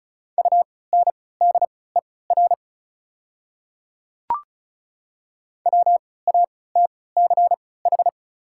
Morse code, heard as a keyed tone, about 34 wpm, spaced as for 20 wpm, 690 Hz. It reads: UNDER WATCH
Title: 3x Character Spacing - Top 300 Words - 35wpm